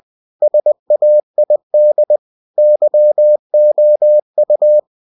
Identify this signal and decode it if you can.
Morse code, heard as a keyed tone, about 20 wpm, 600 Hz. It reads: SAID YOU
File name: cw-books / winnie-the-pooh / winnie-20wpm-page-0020